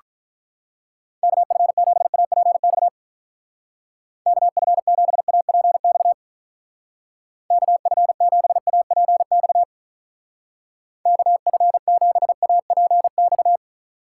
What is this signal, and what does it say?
Morse code, heard as a keyed tone, about 44 wpm, 700 Hz. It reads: KF7APX KF7APX KF7APX KF7APX